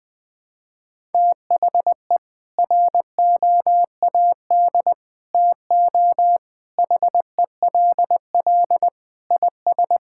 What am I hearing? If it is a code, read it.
Morse code, heard as a keyed tone, about 20 wpm, 700 Hz. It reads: THE ROAD TO HELL IS
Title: Sentences with 8-Word Phrases - Spoken Beginning - 20wpm